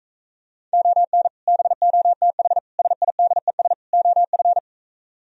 Morse code, heard as a keyed tone, about 45 wpm, 700 Hz. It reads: ON BOTH SIDES OF